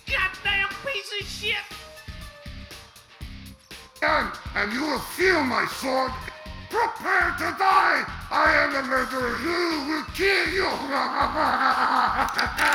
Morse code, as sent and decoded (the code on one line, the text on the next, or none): none